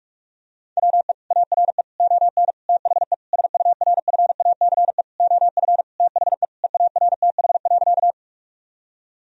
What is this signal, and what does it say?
Morse code, heard as a keyed tone, about 46 wpm, 700 Hz. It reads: WE ARE ON THE SURFACE OF THE EARTH.
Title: Sentences from Top 1000 Words - Review Part 3 - 45wpm